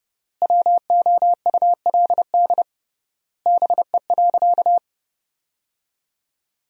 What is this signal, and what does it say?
Morse code, heard as a keyed tone, about 30 wpm, 700 Hz. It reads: WOULD BE.
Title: 3x Word-Spacing - Sentences from Top 200 Words - 30wpm